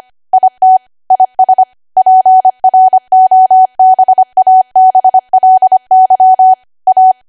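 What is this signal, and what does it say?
Morse code, heard as a keyed tone, about 25 wpm, 750 Hz. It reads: IT IS PROBABLY A
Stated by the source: ARRL 25 WPM code practice archive.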